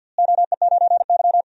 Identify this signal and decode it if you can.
Morse code, heard as a keyed tone, about 52 wpm, 700 Hz. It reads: OE9Y
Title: DXCC-100 Call Signs - 4 Characters - Repeated 2x - 50wpm